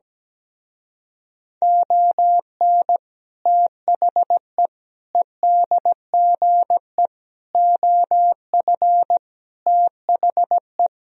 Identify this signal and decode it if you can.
Morse code, heard as a keyed tone, about 17 wpm, 700 Hz. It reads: ON THE EDGE OF THE
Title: Speed-Racing - Sets of 5 Words 17wpm